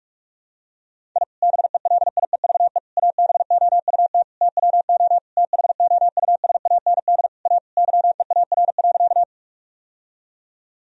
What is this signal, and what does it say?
Morse code, heard as a keyed tone, about 46 wpm, 700 Hz. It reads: I BELIEVE ABOUT TWO THOUSAND A YEAR.
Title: Sentences from Top 1000 Words - Review Part 3 - 45wpm